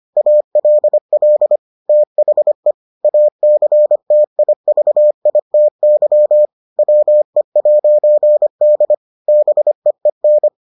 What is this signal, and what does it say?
Morse code, heard as a keyed tone, about 25 wpm, 600 Hz. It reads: ALL THE ACTIVITY WE'D BEEN